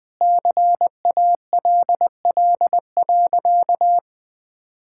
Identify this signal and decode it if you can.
Morse code, heard as a keyed tone, about 20 wpm, 700 Hz. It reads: CALL.
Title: Sentences with 3-Word Phrases - Spoken Beginning - 20wpm